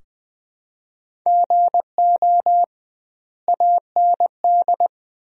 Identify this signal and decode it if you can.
Morse code, heard as a keyed tone, about 20 wpm, 700 Hz. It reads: GO AND
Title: 2x Word-Spacing - Sentences from Top 500 Words - 20wpm